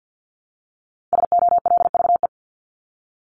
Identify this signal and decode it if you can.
Morse code, heard as a keyed tone, about 50 wpm, 700 Hz. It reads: SOLVE
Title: Top 751-1000 Words 50wpm